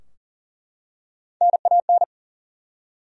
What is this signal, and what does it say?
Morse code, heard as a keyed tone, about 40 wpm, 700 Hz. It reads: NAN
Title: USA Common Names - Max 3 Letters - 40wpm